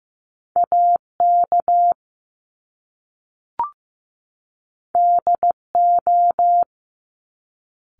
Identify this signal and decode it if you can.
Morse code, heard as a keyed tone, about 15 wpm, 700 Hz. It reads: AK DO
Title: Top Two Letter Combinations in Words 15wpm